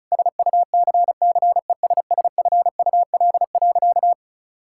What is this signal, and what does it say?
Morse code, heard as a keyed tone, about 36 wpm, 700 Hz. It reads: SUCCESSFUL.
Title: Sentences from Top 1000 Words - Review Part 6 - 35wpm